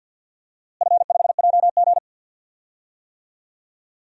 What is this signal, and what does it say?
Morse code, heard as a keyed tone, about 53 wpm, 700 Hz. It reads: F5JG
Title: DXCC-100 Call Signs - 4 Characters - Repeated 2x - 50wpm